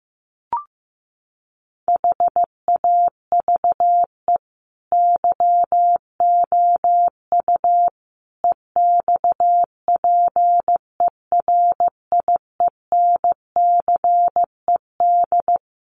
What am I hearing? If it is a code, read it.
Morse code, heard as a keyed tone, about 15 wpm, 700 Hz. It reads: HAVE YOU EXPERIENCED
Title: Sentences from Top 1100 Words 15wpm